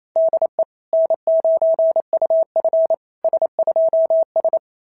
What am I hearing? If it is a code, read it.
Morse code, heard as a keyed tone, about 25 wpm, 650 Hz. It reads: DE N9UF S2S
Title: SOTA QSO Practice 28wpm